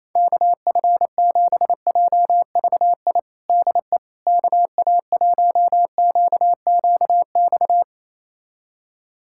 Morse code, heard as a keyed tone, about 28 wpm, 700 Hz. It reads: KF7JVI DE KA1QQX
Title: USA Callsign DE Callsign - 6 Characters - 28wpm